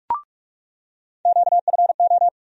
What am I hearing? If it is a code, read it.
Morse code, heard as a keyed tone, about 45 wpm, 700 Hz. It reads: QFO
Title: USA Call Sign Postfixes - 3 Letters 45wpm